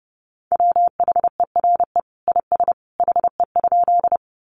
Morse code, heard as a keyed tone, about 30 wpm, 700 Hz. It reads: WHERE IS HE?
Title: Sentences from Top 200 Words 30wpm